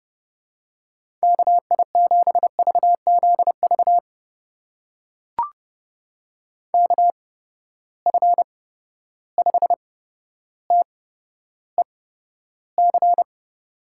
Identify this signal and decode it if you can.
Morse code, heard as a keyed tone, about 30 wpm, spaced as for 13 wpm, 700 Hz. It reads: KI7VZV KF5TEC
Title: USA Call Signs - 6 Characters - 8x Character Spacing - 30wpm